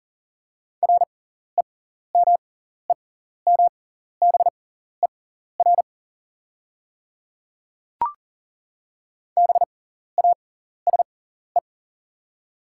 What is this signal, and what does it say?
Morse code, heard as a keyed tone, about 38 wpm, spaced as for 14 wpm, 700 Hz. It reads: REMEMBER BASE
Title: 6x Character Spacing - Top 500 Words - 40wpm